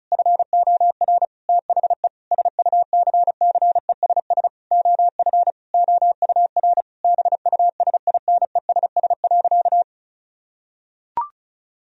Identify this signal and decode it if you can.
Morse code, heard as a keyed tone, about 36 wpm, 700 Hz. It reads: FOR THE SUCCESS OF OUR BUSINESS.
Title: Sentences from Top 1000 Words - Review Part 4 - 35wpm